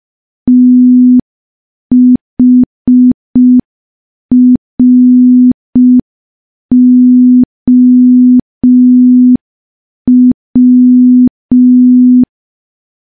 Morse code, heard as a keyed tone, about 5 wpm, 250 Hz. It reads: THROW
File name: morse / morse-training-05WPM